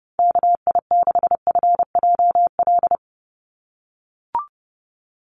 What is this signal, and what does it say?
Morse code, heard as a keyed tone, about 30 wpm, 700 Hz. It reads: KI6FJL
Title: USA Call Signs 30wpm